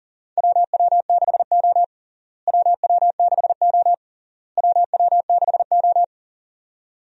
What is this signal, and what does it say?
Morse code, heard as a keyed tone, about 40 wpm, 700 Hz. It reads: WW6O WW6O WW6O
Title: USA Call Signs - 4 Characters - Repeated 3x - 40wpm